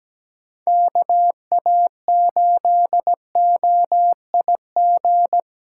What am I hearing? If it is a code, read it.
Morse code, heard as a keyed tone, about 17 wpm, 700 Hz. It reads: KA8OIG
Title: USA Callsign DE Callsign - 6 Characters - 17wpm